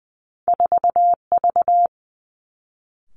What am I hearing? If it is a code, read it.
Morse code, heard as a keyed tone, about 20 wpm, 700 Hz. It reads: 4V
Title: Two Challenging Letter-Number Combinations 20wpm